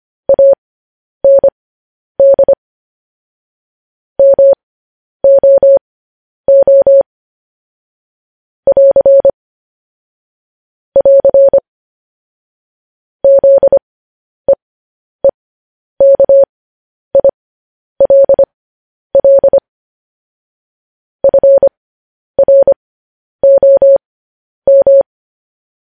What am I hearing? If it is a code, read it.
Morse code, heard as a keyed tone, about 25 wpm, spaced as for 10 wpm, 550 Hz. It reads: AND MOO <AR> <AR> ZEEKILL FROM